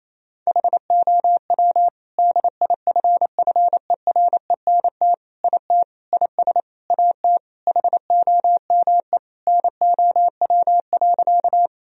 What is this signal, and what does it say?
Morse code, heard as a keyed tone, about 27 wpm, 700 Hz. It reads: HOW DIFFERENT IT IS AT HOME NOW.
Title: Sentences from Top 1000 Words - Review Part 3 - 28wpm